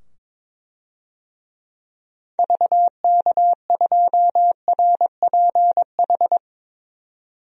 Morse code, heard as a keyed tone, about 22 wpm, 700 Hz. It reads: VK2RPH
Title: DXCC-100 Call Signs - 6 Characters - 22wpm